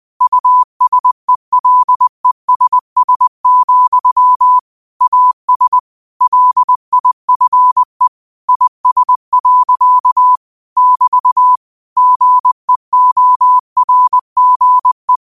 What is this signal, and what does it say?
Morse code, heard as a keyed tone, about 20 wpm, 1000 Hz. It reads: USELESS, AS LIFE IS. <BT> GEORGE
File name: qotdfiles / 20wpm